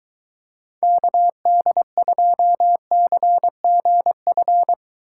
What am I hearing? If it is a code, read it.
Morse code, heard as a keyed tone, about 23 wpm, 700 Hz. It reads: KD2CGF